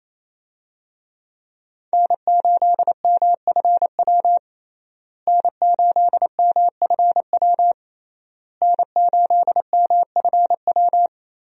Morse code, heard as a keyed tone, about 28 wpm, 700 Hz. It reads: N8MFW N8MFW N8MFW
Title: USA Call Signs - 5 Characters - Repeated 3x - 28wpm